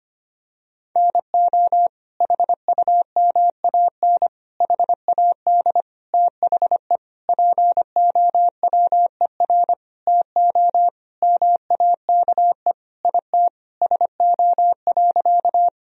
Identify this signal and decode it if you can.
Morse code, heard as a keyed tone, about 25 wpm, 700 Hz. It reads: NO HUMAN HAD THE POWER TO MAKE IT SO.
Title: Sentences from Top 1000 Words - Review Part 3 - 25wpm